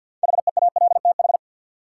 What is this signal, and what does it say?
Morse code, heard as a keyed tone, about 53 wpm, 700 Hz. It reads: HEALTH
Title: Sets of 5 Words - New Word Drill - 50wpm